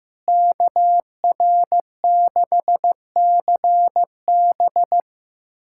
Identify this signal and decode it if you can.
Morse code, heard as a keyed tone, about 15 wpm, 700 Hz. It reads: KR6CB
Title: USA Callsign DE Callsign - 5 Characters - 15wpm